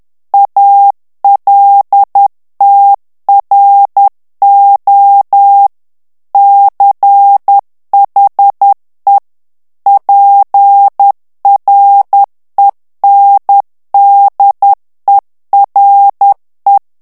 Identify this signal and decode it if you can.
Morse code, heard as a keyed tone, about 10 wpm, 800 Hz. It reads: ALTRO CHE PRENDERE
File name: ITALIEN texte clair 2 vitesse 10 mots minute